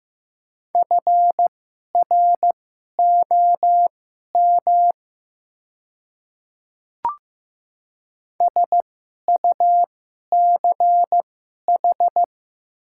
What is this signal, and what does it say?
Morse code, heard as a keyed tone, about 15 wpm, spaced as for 11 wpm, 700 Hz. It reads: FROM SUCH
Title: 2x Character Spacing - Top 300 Words - 15wpm